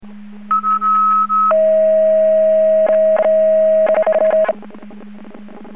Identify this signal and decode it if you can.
Two-tone page, heard as a one-second tone, then a three-second tone, about 1290 then 640 Hz. It is heard from 0.5 to 4.5 s.